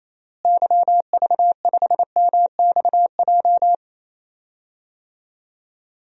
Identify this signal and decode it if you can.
Morse code, heard as a keyed tone, about 28 wpm, 700 Hz. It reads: YV5MXJ
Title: DXCC-100 Call Signs - 6 Characters - Repeated 2x - 28wpm